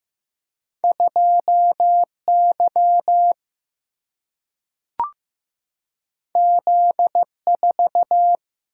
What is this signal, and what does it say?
Morse code, heard as a keyed tone, about 15 wpm, 700 Hz. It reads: 2 Y Z 4
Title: Single-Letter Single-Number 15wpm